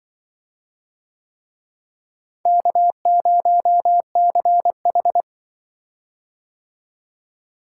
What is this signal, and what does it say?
Morse code, heard as a keyed tone, about 24 wpm, 700 Hz. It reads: K0CH
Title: Speed-Racing - Four Letter-Number Combinations - 20wpm